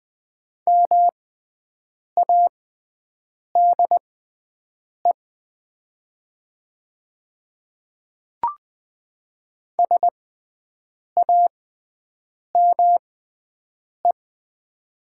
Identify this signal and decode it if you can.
Morse code, heard as a keyed tone, about 20 wpm, spaced as for 7 wpm, 700 Hz. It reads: MADE SAME
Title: 6x Character Spacing - Top 300 Words - 20wpm